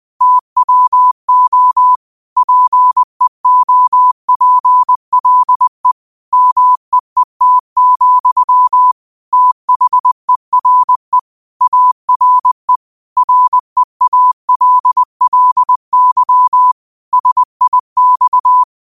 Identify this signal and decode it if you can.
Morse code, heard as a keyed tone, about 20 wpm, 1000 Hz. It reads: TWO PEOPLE MEET, THERE ARE REALLY SIX
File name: qotdfiles / 20wpm